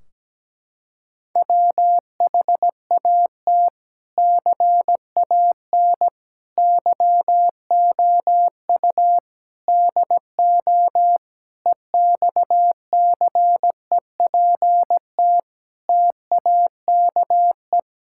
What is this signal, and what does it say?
Morse code, heard as a keyed tone, about 17 wpm, 700 Hz. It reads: WHAT CAN YOU DO EXCEPT TAKE
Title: Sentences from Top 1600 Words 17wpm